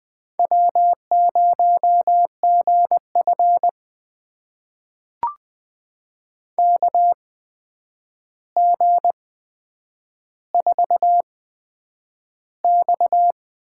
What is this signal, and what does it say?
Morse code, heard as a keyed tone, about 20 wpm, spaced as for 9 wpm, 700 Hz. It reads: W0GF KG4X
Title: USA Call Signs - 4 Characters - 8x Character Spacing - 20wpm